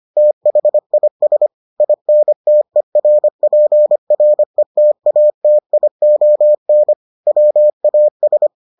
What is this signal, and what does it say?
Morse code, heard as a keyed tone, about 25 wpm, 600 Hz. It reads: THIS INTERPRETATION WAS